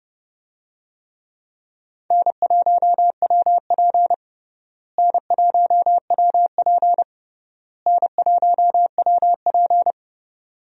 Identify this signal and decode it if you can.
Morse code, heard as a keyed tone, about 30 wpm, 700 Hz. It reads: N1WP N1WP N1WP
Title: USA Call Signs - 4 Characters - Repeated 3x - 30wpm